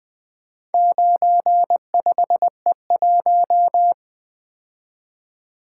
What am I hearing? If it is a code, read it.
Morse code, heard as a keyed tone, about 23 wpm, 700 Hz. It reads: 95E1
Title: Speed-Racing - Four Letter-Number Combinations - 20wpm